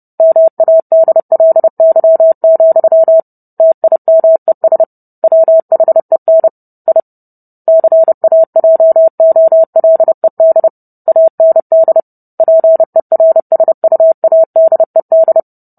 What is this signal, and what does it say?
Morse code, heard as a keyed tone, about 30 wpm, 650 Hz. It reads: MADLY, TIMES WHEN I CAJOLED AND PERSUADED